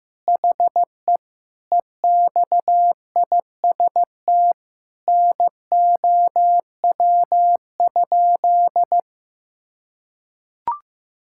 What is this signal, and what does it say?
Morse code, heard as a keyed tone, about 15 wpm, 700 Hz. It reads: HE EXIST NOW?
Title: Sentences from Top 1000 Words - Review Part 5 - 15wpm